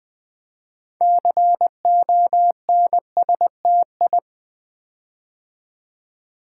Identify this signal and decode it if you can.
Morse code, heard as a keyed tone, about 20 wpm, 700 Hz. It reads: CONSTI
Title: Build A Word - 6 Letters At A Time - 12-Letter Words - 20wpm